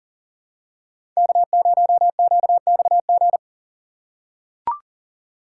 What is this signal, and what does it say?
Morse code, heard as a keyed tone, about 40 wpm, 700 Hz. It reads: K0QXG
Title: USA Call Signs - 5 Characters - 3x Character Spacing - 40wpm